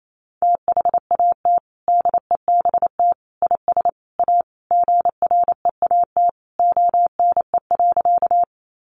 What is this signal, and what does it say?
Morse code, heard as a keyed tone, about 28 wpm, 700 Hz. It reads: THAT DEBT IS A GREAT ONE.
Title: Sentences from Top 1300 Words 28wpm